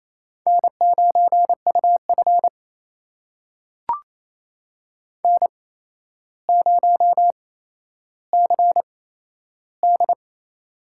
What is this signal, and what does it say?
Morse code, heard as a keyed tone, about 28 wpm, spaced as for 12 wpm, 700 Hz. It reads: N9UF N0CD
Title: USA Call Signs - 4 Characters - 8x Character Spacing - 28wpm